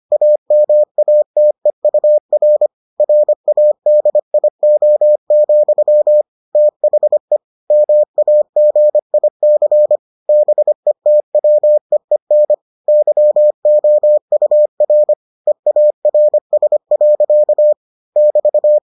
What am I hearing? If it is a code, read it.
Morse code, heard as a keyed tone, about 25 wpm, 600 Hz. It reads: AMATEUR RADIO, THE MAGIC BETWEEN YOUR EARS. <BT>